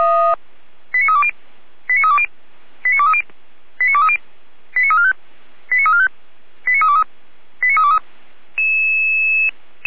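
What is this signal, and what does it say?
Five-tone selective call (ZVEI), heard as quick runs of five tones, tones of about 80 ms. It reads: 89320 89320 89230 89230 89345 89345 89323 89323
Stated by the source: ZVEI five-tone selective call 89015 received by SDR.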